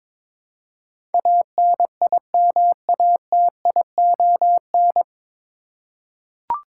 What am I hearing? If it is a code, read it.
Morse code, heard as a keyed tone, about 22 wpm, 700 Hz. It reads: ANIMATION
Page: Morse Code Ninja practice 22wpm